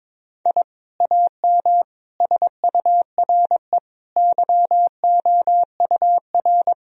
Morse code, heard as a keyed tone, about 22 wpm, 700 Hz. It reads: I AM SURE YOUR
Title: Sentences from Top 1000 Words - Review Part 3 - 22wpm